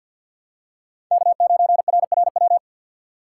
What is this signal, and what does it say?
Morse code, heard as a keyed tone, about 48 wpm, 700 Hz. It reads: K9RRW
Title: USA Call Signs - 5 Characters - 4x Character Spacing - 50wpm